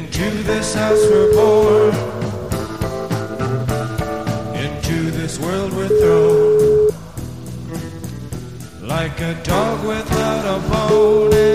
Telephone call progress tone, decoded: ringback tone

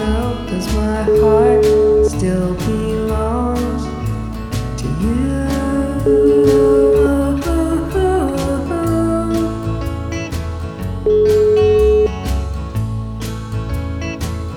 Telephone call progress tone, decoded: ringback tone

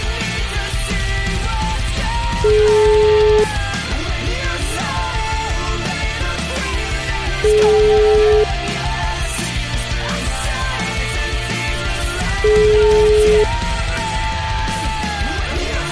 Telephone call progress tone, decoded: ringback tone

